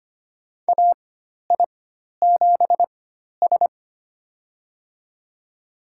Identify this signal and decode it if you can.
Morse code, heard as a keyed tone, about 25 wpm, spaced as for 12 wpm, 700 Hz. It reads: AI7S
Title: USA Call Signs - 4 Characters - 4x Character Spacing - 25wpm